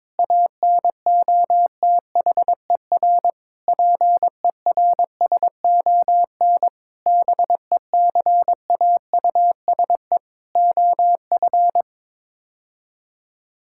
Morse code, heard as a keyed tone, about 22 wpm, 700 Hz. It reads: ANOTHER PERSON BECAUSE OF
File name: Sets of 9 Words - 22wpm